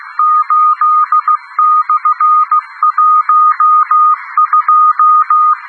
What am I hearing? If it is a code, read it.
Morse code, heard as a keyed tone, about 16 wpm, 1230 Hz. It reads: 8/12
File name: europe / d-2816-ruegen radio dhs-whe-1992